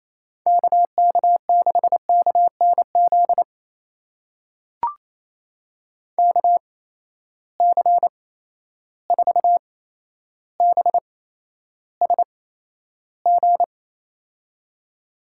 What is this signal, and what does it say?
Morse code, heard as a keyed tone, about 28 wpm, spaced as for 12 wpm, 700 Hz. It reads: KK6KNZ KC4BSG
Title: USA Call Signs - 6 Characters - 8x Character Spacing - 28wpm